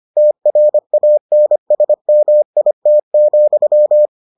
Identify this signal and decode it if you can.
Morse code, heard as a keyed tone, about 25 wpm, 600 Hz. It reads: TRANSMIT,